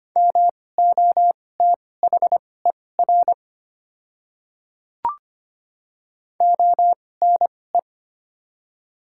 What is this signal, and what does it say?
Morse code, heard as a keyed tone, about 25 wpm, spaced as for 18 wpm, 700 Hz. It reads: MOTHER ONE
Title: 2x Character Spacing - Top 300 Words - 25wpm